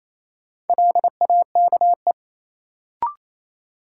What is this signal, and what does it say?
Morse code, heard as a keyed tone, about 28 wpm, 700 Hz. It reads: LAKE
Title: Sentences from Top 1100 Words - New Word Drill Alt - 28wpm